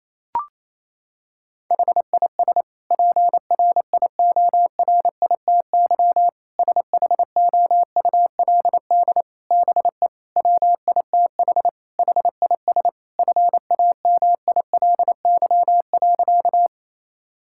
Morse code, heard as a keyed tone, about 28 wpm, 700 Hz. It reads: HIS PRIORITY SHOULD BE WITH HIS FAMILY.
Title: Sentences from Top 1600 Words 28wpm